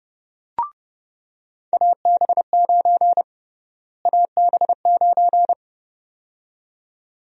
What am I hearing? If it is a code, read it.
Morse code, heard as a keyed tone, about 32 wpm, 700 Hz. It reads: AB9 AB9
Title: USA Call Sign Prefixes - Repeated 2x - 30wpm